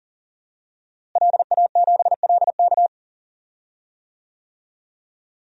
Morse code, heard as a keyed tone, about 40 wpm, 700 Hz. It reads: LA7LK
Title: DXCC-100 Call Signs - 5 Characters - Repeated 2x - 40wpm